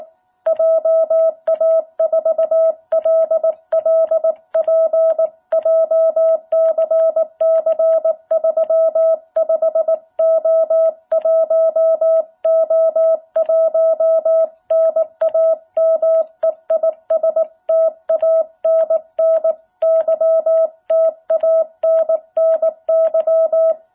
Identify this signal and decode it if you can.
Morse code, heard as a keyed tone, about 19 wpm, 660 Hz. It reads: JA4LLP JCC35O1O1 NAME IS TANNY TANNY
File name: jh8cbh / 2021cw